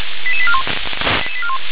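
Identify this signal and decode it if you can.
Five-tone selective call (ZVEI), heard as quick runs of five tones, tones of about 70 ms. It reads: 88841
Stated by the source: ZVEI five-tone selective call 88981 received by SDR.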